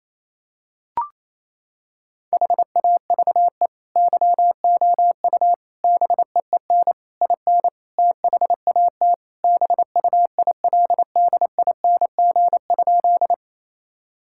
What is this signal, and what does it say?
Morse code, heard as a keyed tone, about 28 wpm, 700 Hz. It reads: HAVE YOU BEEN IN THAT BUILDING?
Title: Sentences from Top 1000 Words - Review Part 5 - 28wpm